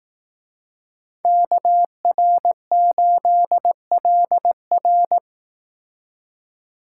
Morse code, heard as a keyed tone, about 18 wpm, 700 Hz. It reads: KR8LR
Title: Speed Racing - USA 2x2 Call Signs 15wpm